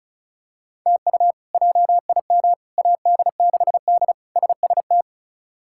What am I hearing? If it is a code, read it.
Morse code, heard as a keyed tone, about 34 wpm, 700 Hz. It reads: TU JIM AD6D SST
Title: K1USN SST QSO Practice - 35wpm